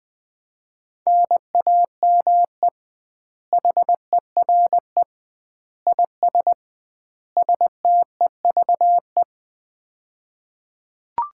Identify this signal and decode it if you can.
Morse code, heard as a keyed tone, about 20 wpm, 700 Hz. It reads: NAME HERE IS STEVE
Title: Common QSO Phrases - 2x Word-Spacing - 20wpm